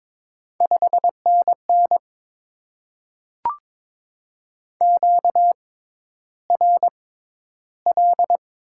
Morse code, heard as a keyed tone, about 22 wpm, spaced as for 11 wpm, 700 Hz. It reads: 5NN QRL
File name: Top 20 QSO Elements - 6x Character Spacing - 22wpm